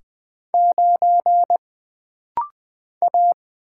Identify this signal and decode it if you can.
Morse code, heard as a keyed tone, about 20 wpm, 700 Hz. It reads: 9 A
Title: Single Letter-Number - Rapid-Fire 20wpm